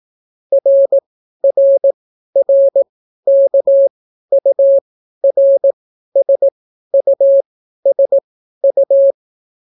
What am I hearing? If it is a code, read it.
Morse code, heard as a keyed tone, about 18 wpm, spaced as for 12 wpm, 550 Hz. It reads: RRRKURSUSU